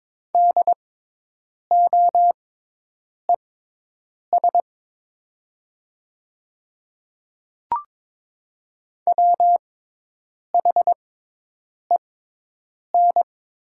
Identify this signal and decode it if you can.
Morse code, heard as a keyed tone, about 22 wpm, spaced as for 8 wpm, 700 Hz. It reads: DOES WHEN
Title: 6x Character Spacing - Top 200 Words - 22wpm